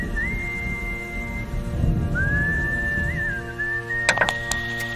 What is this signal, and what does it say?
Dial-up modem handshake.